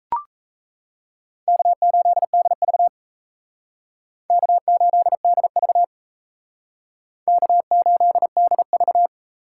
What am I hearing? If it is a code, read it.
Morse code, heard as a keyed tone, about 36 wpm, 700 Hz. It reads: K8DV K8DV K8DV